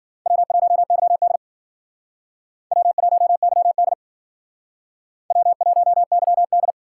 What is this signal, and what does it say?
Morse code, heard as a keyed tone, about 52 wpm, 700 Hz. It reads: W1YD W1YD W1YD